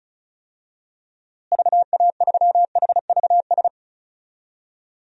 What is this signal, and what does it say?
Morse code, heard as a keyed tone, about 36 wpm, 700 Hz. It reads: VA3HVS
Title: DXCC-100 Call Signs - 6 Characters - Repeated 2x - 35wpm